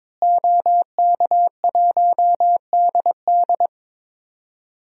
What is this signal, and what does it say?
Morse code, heard as a keyed tone, about 22 wpm, 700 Hz. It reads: OK1DD
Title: DXCC-100 Call Signs - 5 Characters - Repeated 2x - 22wpm